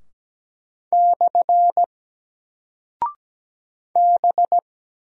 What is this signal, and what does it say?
Morse code, heard as a keyed tone, about 17 wpm, 700 Hz. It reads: / B
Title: Single Character (Letter:Number:Punctuation:Prosign) 17wpm